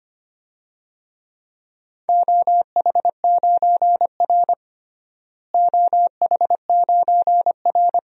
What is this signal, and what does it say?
Morse code, heard as a keyed tone, about 25 wpm, 700 Hz. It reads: OH9R OH9R
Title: DXCC-100 Call Signs - 4 Characters - Repeated 3x - 25wpm